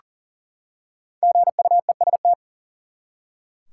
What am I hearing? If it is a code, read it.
Morse code, heard as a keyed tone, about 40 wpm, 700 Hz. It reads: GUEST
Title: Five Letter Words - 40wpm